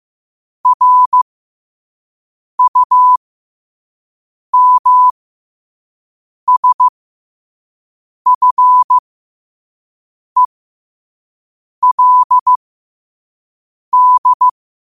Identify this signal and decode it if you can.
Morse code, heard as a keyed tone, about 15 wpm, spaced as for 5 wpm, 1000 Hz. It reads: RUMSFELD
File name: qotdfiles / 5wpm